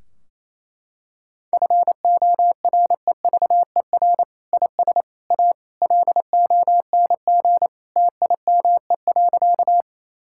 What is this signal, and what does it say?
Morse code, heard as a keyed tone, about 28 wpm, 700 Hz. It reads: FOREVER IS A LONG TIME.
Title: Sentences from Top 1600 Words 28wpm